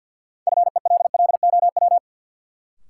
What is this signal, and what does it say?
Morse code, heard as a keyed tone, about 46 wpm, 700 Hz. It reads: FELLOW
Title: Sets of 5 Words Encore - New Word Drill - 50wpm